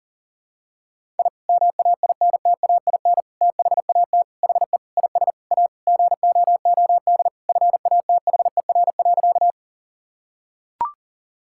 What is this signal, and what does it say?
Morse code, heard as a keyed tone, about 40 wpm, 700 Hz. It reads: I MAINTAIN THAT HE IS A GOOD FATHER.
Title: Sentences from Top 1000 Words - Review Part 5 - 40wpm